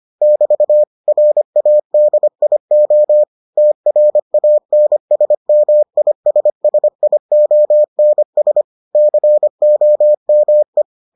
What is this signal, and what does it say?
Morse code, heard as a keyed tone, about 25 wpm, 600 Hz. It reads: <BT> RADIO TRANSMISSIONS COME